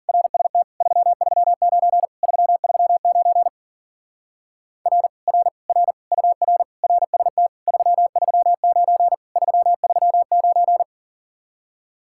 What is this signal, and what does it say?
Morse code, heard as a keyed tone, about 49 wpm, 700 Hz. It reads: RST 339 339 R R R UR RST 339 339
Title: Speed-Racing Common QSO Phrases - 40wpm